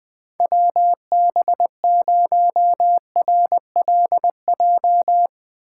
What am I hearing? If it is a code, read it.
Morse code, heard as a keyed tone, about 20 wpm, 700 Hz. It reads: WB0RLJ
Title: 2023 POTA Activator Call Signs - 20wpm